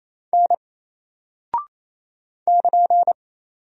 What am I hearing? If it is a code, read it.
Morse code, heard as a keyed tone, about 28 wpm, 700 Hz. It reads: N <KN>
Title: Single Character (Letter:Number:Punctuation:Prosign) 28wpm